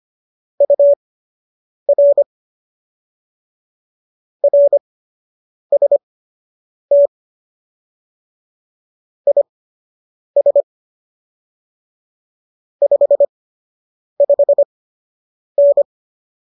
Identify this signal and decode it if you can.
Morse code, heard as a keyed tone, about 25 wpm, spaced as for 8 wpm, 570 Hz. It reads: UR RST IS 55N